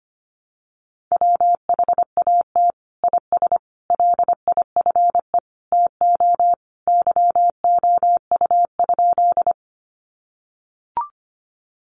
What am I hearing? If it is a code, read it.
Morse code, heard as a keyed tone, about 25 wpm, 700 Hz. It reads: WHAT IS LIFE TO YOU?
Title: Sentences from Top 300 Words 25wpm